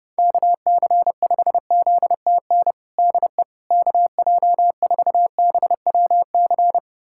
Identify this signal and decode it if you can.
Morse code, heard as a keyed tone, about 30 wpm, 700 Hz. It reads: KC5ZTN DE KJ4BWC
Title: USA Callsign DE Callsign 30wpm